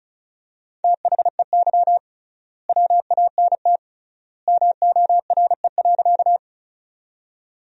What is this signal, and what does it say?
Morse code, heard as a keyed tone, about 36 wpm, 700 Hz. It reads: THEY WANT MORE.
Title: 3x Word-Spacing - Sentences from Top 200 Words - 35wpm